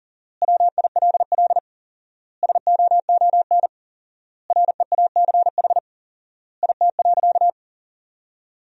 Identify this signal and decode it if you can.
Morse code, heard as a keyed tone, about 37 wpm, 700 Hz. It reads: WILL SOON REACH IT.
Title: 4x Word-Spacing - Sentences from Top 400 Words - 40wpm